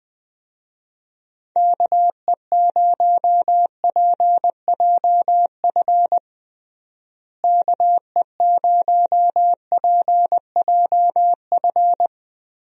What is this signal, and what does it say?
Morse code, heard as a keyed tone, about 20 wpm, 700 Hz. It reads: KE0PJF KE0PJF
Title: USA Call Signs - 6 Characters - Repeated 3x - 20wpm